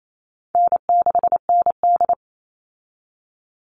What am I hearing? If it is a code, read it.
Morse code, heard as a keyed tone, about 28 wpm, 700 Hz. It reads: N6ND
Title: All Cut Numbers - 4-Digit 28wpm